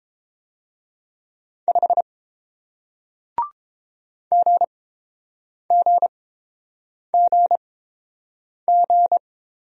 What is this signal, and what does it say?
Morse code, heard as a keyed tone, about 31 wpm, spaced as for 7 wpm, 700 Hz. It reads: 5 GGGG